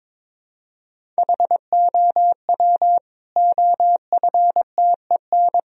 Morse code, heard as a keyed tone, about 22 wpm, 700 Hz. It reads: HOW OFTEN
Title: Sentences from Top 1000 Words - Review Part 1 - 22wpm